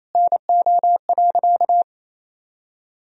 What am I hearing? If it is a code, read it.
Morse code, heard as a keyed tone, about 28 wpm, 700 Hz. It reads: NO.
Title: Sentences from Top 1000 Words - Review Part 4 - 28wpm